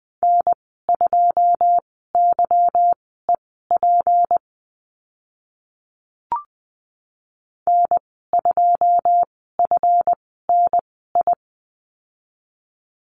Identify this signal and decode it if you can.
Morse code, heard as a keyed tone, about 20 wpm, spaced as for 15 wpm, 700 Hz. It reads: N2YEP N2FNI
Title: 167 Lesson 27g - Number 7 - Call Signs 20wpm